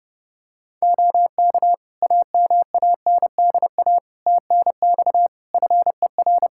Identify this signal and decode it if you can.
Morse code, heard as a keyed tone, about 32 wpm, 700 Hz. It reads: OK AMANDA TNX FER
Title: K1USN SST QSO Practice 30wpm